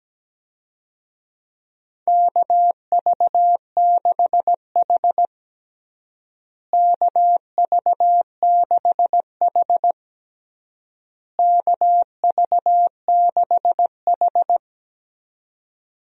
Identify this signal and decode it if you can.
Morse code, heard as a keyed tone, about 17 wpm, 700 Hz. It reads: KV6H KV6H KV6H
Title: USA Call Signs - 4 Characters - Repeated 3x - 17wpm